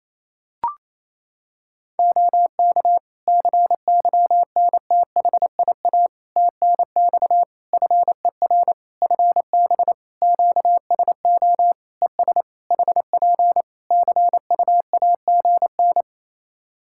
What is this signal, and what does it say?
Morse code, heard as a keyed tone, about 27 wpm, 700 Hz. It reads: OK CYNTHIA TNX FER FB QSO ES HP CUAGN
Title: K1USN SST QSO Practice 28wpm